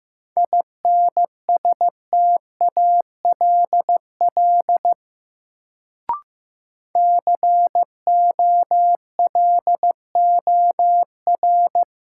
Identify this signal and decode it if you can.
Morse code, heard as a keyed tone, about 15 wpm, 700 Hz. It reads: INSTALL COLOR
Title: Sentences from Top 1600 Words - New Word Drill Alt - 15wpm